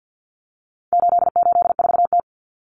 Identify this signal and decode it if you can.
Morse code, heard as a keyed tone, about 50 wpm, 700 Hz. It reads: 884T
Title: Common Cut Numbers - 4-Digit 50wpm